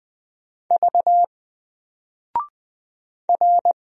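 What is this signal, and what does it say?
Morse code, heard as a keyed tone, about 20 wpm, 700 Hz. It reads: V R
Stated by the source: Single letter and number Morse drill at 20 wpm, from N4LCD.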